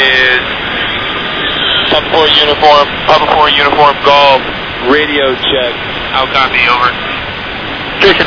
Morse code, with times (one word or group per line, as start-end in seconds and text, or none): none